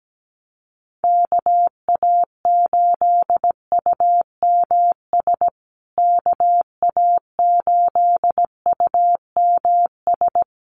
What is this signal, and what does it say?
Morse code, heard as a keyed tone, about 17 wpm, 700 Hz. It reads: KA8UMS KA8UMS
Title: USA Call Signs Repeated 2x 17wpm